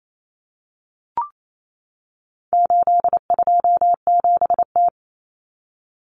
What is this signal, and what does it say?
Morse code, heard as a keyed tone, about 28 wpm, 700 Hz. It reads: 827T
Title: Common Cut Numbers - 4-Digit 28wpm